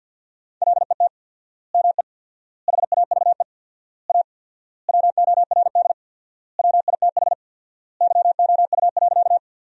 Morse code, heard as a keyed tone, about 50 wpm, 700 Hz. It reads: LET ME HAVE A WORD WITH YOU.
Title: 4x Word-Spacing - Sentences from Top 300 Words - 50wpm